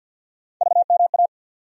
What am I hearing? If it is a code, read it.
Morse code, heard as a keyed tone, about 48 wpm, 700 Hz. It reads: VMA
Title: USA Call Sign Postfixes - 3 Letters 50wpm